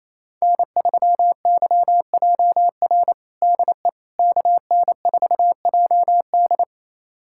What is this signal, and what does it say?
Morse code, heard as a keyed tone, about 28 wpm, 700 Hz. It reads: N3YJR DE KN4JD
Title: USA Callsign DE Callsign - 5 Characters - 28wpm